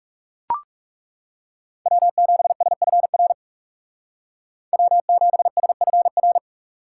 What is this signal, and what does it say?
Morse code, heard as a keyed tone, about 39 wpm, 700 Hz. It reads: W7SFR W7SFR